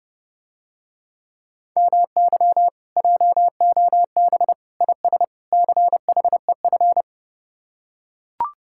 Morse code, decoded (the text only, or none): MY JOB IS CHEF